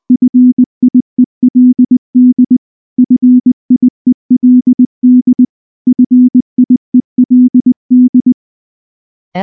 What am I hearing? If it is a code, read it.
Morse code, heard as a keyed tone, about 20 wpm, 250 Hz. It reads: FIELD FIELD FIELD
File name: morse / morse-training-20WPM-5